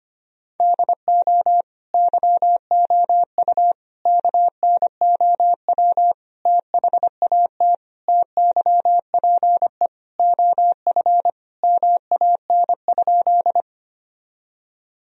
Morse code, heard as a keyed tone, about 25 wpm, 700 Hz. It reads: DO YOU KNOW THAT TYPE OF MAN?
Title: Sentences from Top 1000 Words - Review Part 3 - 25wpm